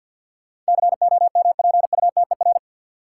Morse code, heard as a keyed tone, about 49 wpm, 700 Hz. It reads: COMPUTER
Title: Sentences from Top 1000 Words - Review Part 3 - New Word Drill - 50wpm